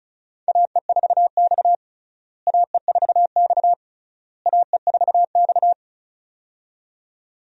Morse code, heard as a keyed tone, about 36 wpm, 700 Hz. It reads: AE4X AE4X AE4X
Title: USA Call Signs - 4 Characters - Repeated 3x - 35wpm